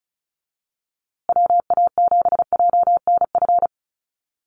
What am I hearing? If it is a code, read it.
Morse code, heard as a keyed tone, about 34 wpm, 700 Hz. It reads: WA7JNF
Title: USA Call Signs Repeated 3x 35wpm